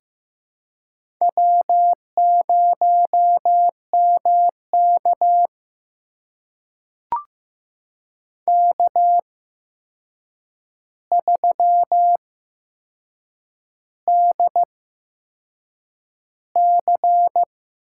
Morse code, heard as a keyed tone, about 15 wpm, spaced as for 6 wpm, 700 Hz. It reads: W0MK K3DC
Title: USA Call Signs - 4 Characters - 8x Character Spacing - 15wpm